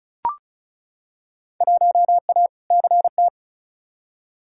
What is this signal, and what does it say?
Morse code, heard as a keyed tone, about 35 wpm, 700 Hz. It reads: 1A CT
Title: Field Day Exchanges Encore - 35wpm